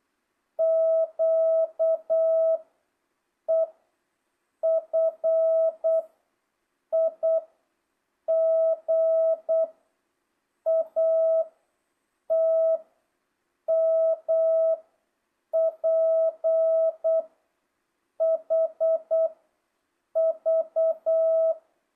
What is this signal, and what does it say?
Morse code, heard as a keyed tone, about 8 wpm, 660 Hz. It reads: Q E F I G A T M P H V
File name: jh8cbh / 2025cw